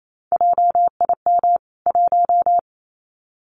Morse code, heard as a keyed tone, about 28 wpm, 700 Hz. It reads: JIM 1
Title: CWT Exchange 28wpm